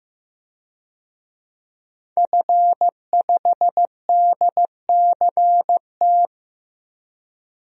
Morse code, heard as a keyed tone, about 15 wpm, 700 Hz. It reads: F5DCT
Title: DXCC-100 Call Signs - 5 Characters - 15wpm